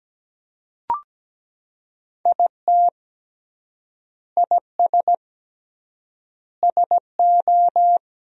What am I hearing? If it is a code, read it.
Morse code, heard as a keyed tone, about 17 wpm, 700 Hz. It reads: IT IS SO
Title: 3x Word-Spacing - Sentences from Top 300 Words - 17wpm